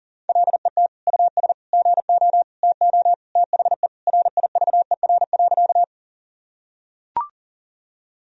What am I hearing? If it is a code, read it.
Morse code, heard as a keyed tone, about 40 wpm, 700 Hz. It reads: LET US GO TO THE RIVER.
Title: Sentences from Top 1000 Words - Review Part 1 - 40wpm